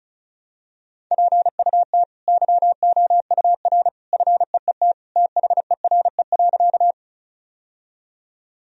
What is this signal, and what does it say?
Morse code, heard as a keyed tone, about 36 wpm, 700 Hz. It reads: PUT YOUR FEET THERE.
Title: Sentences from Top 1000 Words - Review Part 1 - 35wpm